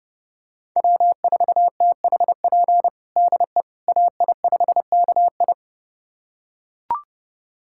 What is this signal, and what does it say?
Morse code, heard as a keyed tone, about 30 wpm, 700 Hz. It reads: W4THP DE AI5KI
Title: USA Callsign DE Callsign - 5 Characters - 30wpm